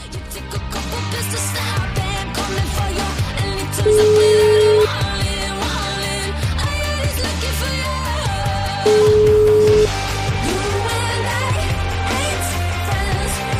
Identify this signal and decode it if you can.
Ringback tone (Europe, 425 Hz), with noise.